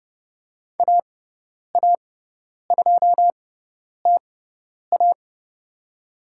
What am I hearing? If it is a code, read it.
Morse code, heard as a keyed tone, about 30 wpm, spaced as for 10 wpm, 700 Hz. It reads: AA2TA
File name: Lesson 18g - Letter m - Call Signs 30wpm